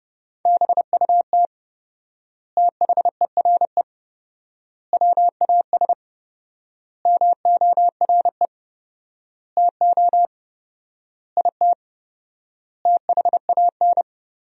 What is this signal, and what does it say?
Morse code, heard as a keyed tone, about 30 wpm, 700 Hz. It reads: BUT THERE WAS MORE TO IT THAN
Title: 4x Word-Spacing - Sentences from Top 100 Words - 30wpm